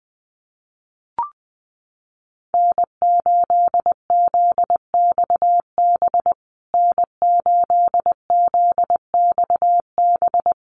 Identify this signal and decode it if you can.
Morse code, heard as a keyed tone, about 20 wpm, 700 Hz. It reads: N8ZXB N8ZXB
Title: USA Call Signs Repeated 3x 20wpm